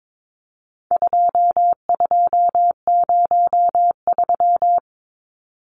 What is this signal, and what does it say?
Morse code, heard as a keyed tone, about 22 wpm, 700 Hz. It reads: 2203
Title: Numbers - 4-Digit 22wpm